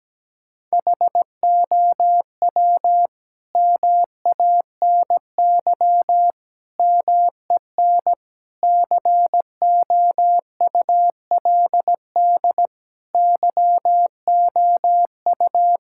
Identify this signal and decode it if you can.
Morse code, heard as a keyed tone, about 17 wpm, 700 Hz. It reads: HOW MANY MEN COULD YOU
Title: Sentences from Top 1600 Words 17wpm